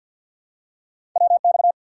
Morse code, heard as a keyed tone, about 53 wpm, 700 Hz. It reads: WX
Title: Top 90 QSO Elements 50wpm